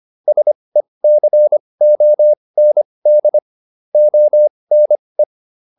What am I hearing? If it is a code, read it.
Morse code, heard as a keyed tone, about 25 wpm, spaced as for 20 wpm, 600 Hz. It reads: SECOND ONE